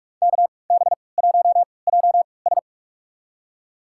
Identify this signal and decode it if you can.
Morse code, heard as a keyed tone, about 43 wpm, spaced as for 25 wpm, 700 Hz. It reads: KB1JS
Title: USA Call Signs - 5 Characters - 3x Character Spacing - 45wpm